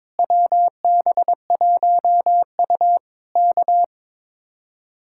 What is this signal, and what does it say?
Morse code, heard as a keyed tone, about 22 wpm, 700 Hz. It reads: WB1U K
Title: Speed-Racing Common QSO Phrases - 22wpm